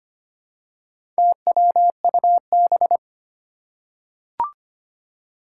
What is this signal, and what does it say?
Morse code, heard as a keyed tone, about 25 wpm, 700 Hz. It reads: TWUB
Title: Four Challenging Letter-Number Combinations 25wpm